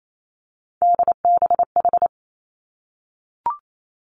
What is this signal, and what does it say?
Morse code, heard as a keyed tone, about 28 wpm, 700 Hz. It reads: DBH
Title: Three Challenging Letter-Number Combinations 28wpm